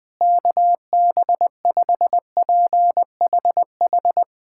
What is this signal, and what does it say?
Morse code, heard as a keyed tone, about 20 wpm, 700 Hz. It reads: KB5PHH
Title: USA Callsign DE Callsign 20wpm